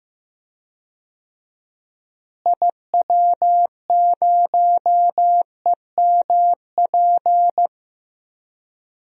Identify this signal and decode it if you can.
Morse code, heard as a keyed tone, about 15 wpm, 700 Hz. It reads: IW0EMP
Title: DXCC-100 Call Signs - 6 Characters - Repeated 3x - 15wpm